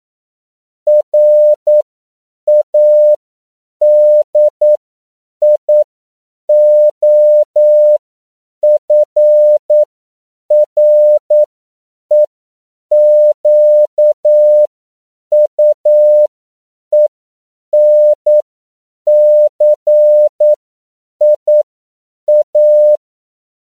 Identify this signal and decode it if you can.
Morse code, heard as a keyed tone, about 9 wpm, spaced as for 7 wpm, 600 Hz. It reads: RADIOFREQUENCIA
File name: CW / 10PPM